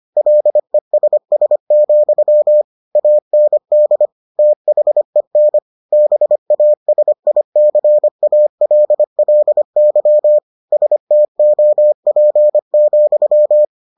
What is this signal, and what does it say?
Morse code, heard as a keyed tone, about 25 wpm, 600 Hz. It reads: LESS, AND THEN BASICALLY STOP,